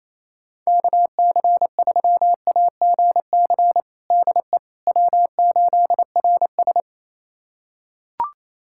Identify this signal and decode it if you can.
Morse code, heard as a keyed tone, about 28 wpm, 700 Hz. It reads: KC3AGC DE W8RS
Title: USA Callsign DE Callsign 28wpm